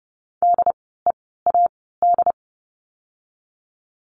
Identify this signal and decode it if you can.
Morse code, heard as a keyed tone, about 30 wpm, 700 Hz. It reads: D E A D
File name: Lesson 16f - Letter c - Words 30wpm